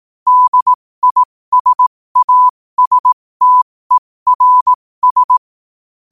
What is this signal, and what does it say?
Morse code, heard as a keyed tone, about 18 wpm, spaced as for 15 wpm, 1000 Hz. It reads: DISASTERS
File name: qotdfiles / 15wpm